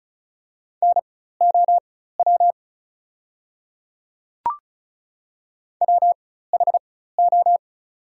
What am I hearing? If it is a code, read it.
Morse code, heard as a keyed tone, about 34 wpm, spaced as for 16 wpm, 700 Hz. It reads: NOW WHO
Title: 4x Character Spacing - Top 100 Words - 35wpm